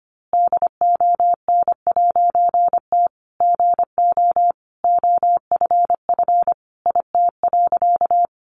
Morse code, heard as a keyed tone, about 25 wpm, 700 Hz. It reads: DON'T GO OFF IT.
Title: A Sound Of Thunder 25wpm